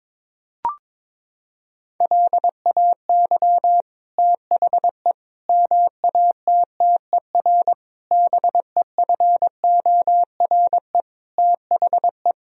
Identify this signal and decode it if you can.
Morse code, heard as a keyed tone, about 21 wpm, 700 Hz. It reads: LAY THE MATTER BEFORE THE
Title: Sentences from Top 1000 Words - Review Part 4 - 22wpm